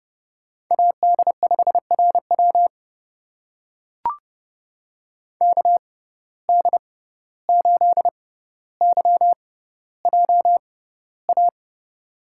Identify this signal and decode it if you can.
Morse code, heard as a keyed tone, about 30 wpm, spaced as for 15 wpm, 700 Hz. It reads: AD5RW KD8YJA
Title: USA Call Signs - 6x Character Spacing - 30wpm